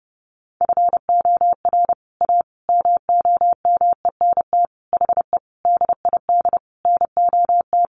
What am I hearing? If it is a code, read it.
Morse code, heard as a keyed tone, about 30 wpm, 700 Hz. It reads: FOR A MOMENT HE DID NOT
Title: Sentences from Top 1100 Words 30wpm